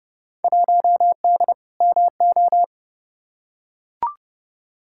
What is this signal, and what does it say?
Morse code, heard as a keyed tone, about 30 wpm, 700 Hz. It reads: 1D MO
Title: Field Day Exchanges - 30wpm